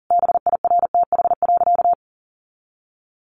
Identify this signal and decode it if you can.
Morse code, heard as a keyed tone, about 40 wpm, 690 Hz. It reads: BIRTH.
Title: Sentences from Top 800 Words 40wpm